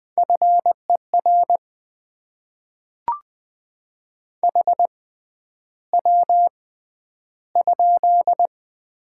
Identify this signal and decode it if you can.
Morse code, heard as a keyed tone, about 20 wpm, spaced as for 10 wpm, 690 Hz. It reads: FER HW?
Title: Top 20 QSO Elements - 6x Character Spacing - 20wpm